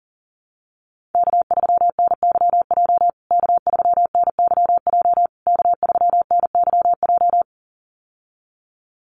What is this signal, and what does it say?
Morse code, heard as a keyed tone, about 40 wpm, 700 Hz. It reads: K3NYJ K3NYJ K3NYJ
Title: USA Call Signs Repeated 3x 40wpm